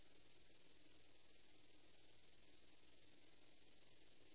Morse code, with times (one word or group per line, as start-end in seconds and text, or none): none